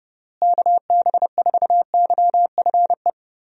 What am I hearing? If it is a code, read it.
Morse code, heard as a keyed tone, about 30 wpm, 700 Hz. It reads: KB4YFE
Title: USA Call Signs - 6 Characters - Repeated 2x - 30wpm